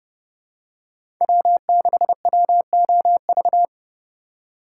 Morse code, heard as a keyed tone, about 30 wpm, 700 Hz. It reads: W6WOV
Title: USA Call Signs - 5 Characters - 4x Character Spacing - 30wpm